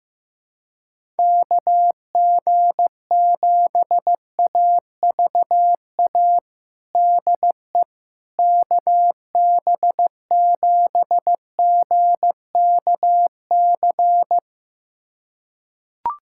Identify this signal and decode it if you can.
Morse code, heard as a keyed tone, about 15 wpm, 700 Hz. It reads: KG7AVA DE KB7GKC
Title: USA Callsign DE Callsign - 6 Characters - 15wpm